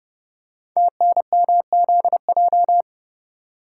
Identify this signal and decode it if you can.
Morse code, heard as a keyed tone, about 30 wpm, 700 Hz. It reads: TNMZJ
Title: Code Groups - 5 Letters - 30wpm